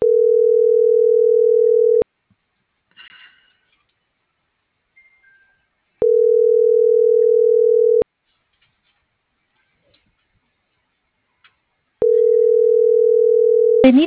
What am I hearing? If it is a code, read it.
Ringback tone (North America), 440 + 480 Hz.
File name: CENTRAL MULTIMODAL / out rodolfo.fuentes